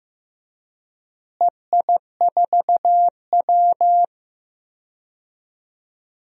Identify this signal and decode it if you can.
Morse code, heard as a keyed tone, about 15 wpm, 700 Hz. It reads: EI4W
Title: DXCC-100 Call Signs - 4 Characters - Repeated 3x - 15wpm